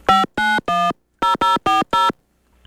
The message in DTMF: 5910040